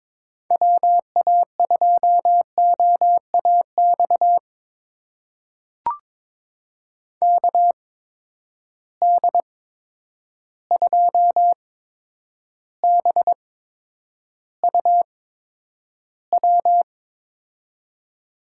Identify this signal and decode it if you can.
Morse code, heard as a keyed tone, about 22 wpm, spaced as for 9 wpm, 700 Hz. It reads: WA2OAX KD2BUW
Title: USA Call Signs - 6 Characters - 8x Character Spacing - 22wpm